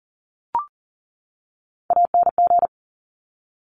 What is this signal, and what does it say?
Morse code, heard as a keyed tone, about 40 wpm, 700 Hz. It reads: ANG
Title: Top Three Letter Combinations in Words 40wpm